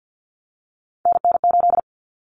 Morse code, heard as a keyed tone, about 50 wpm, 700 Hz. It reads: NN8
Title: Common Cut Numbers - 3-Digit 50wpm